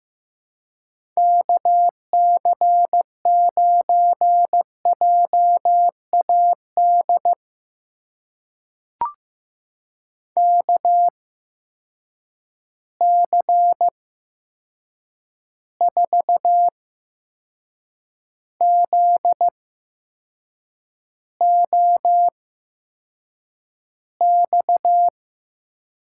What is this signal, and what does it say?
Morse code, heard as a keyed tone, about 15 wpm, spaced as for 6 wpm, 700 Hz. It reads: KC9JAD KC4ZOX
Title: USA Call Signs - 6 Characters - 8x Character Spacing - 15wpm